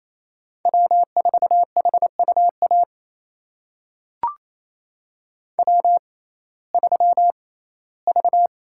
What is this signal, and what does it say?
Morse code, heard as a keyed tone, about 28 wpm, spaced as for 17 wpm, 700 Hz. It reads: W4HUA W3V
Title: USA Call Signs - 6x Character Spacing - 28wpm